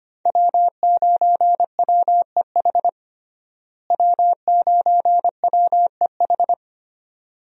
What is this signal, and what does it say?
Morse code, heard as a keyed tone, about 25 wpm, 700 Hz. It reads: W9WEH W9WEH
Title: USA Call Signs - 5 Characters - Repeated 2x - 25wpm